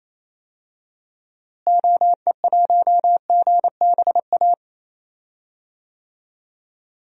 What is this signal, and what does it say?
Morse code, heard as a keyed tone, about 28 wpm, 700 Hz. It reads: OE1GBA
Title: DXCC-100 Call Signs - 6 Characters - Repeated 3x - 28wpm